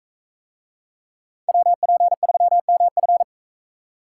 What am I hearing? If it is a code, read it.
Morse code, heard as a keyed tone, about 42 wpm, 700 Hz. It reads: WP3MF